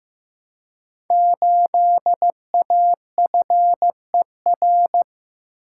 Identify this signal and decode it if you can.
Morse code, heard as a keyed tone, about 15 wpm, 700 Hz. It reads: 8AFER
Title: Code Groups - 5 Letter:Numbers - 15wpm